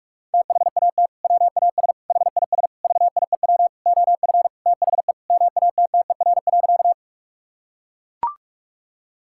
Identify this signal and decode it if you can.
Morse code, heard as a keyed tone, about 44 wpm, 700 Hz. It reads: THAT WAS HIS VIEW OF THE MATTER.
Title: Sentences from Top 1000 Words - Review Part 6 - 45wpm